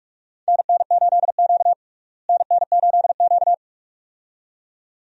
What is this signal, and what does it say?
Morse code, heard as a keyed tone, about 41 wpm, 700 Hz. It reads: NN8Q NN8Q
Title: USA Call Signs - 4 Characters - Repeated 2x - 45wpm